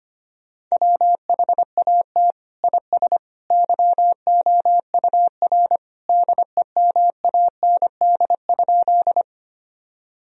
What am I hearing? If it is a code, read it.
Morse code, heard as a keyed tone, about 25 wpm, 700 Hz. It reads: WHAT IS YOUR DEMAND?
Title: Sentences from Top 1000 Words - Review Part 4 - 25wpm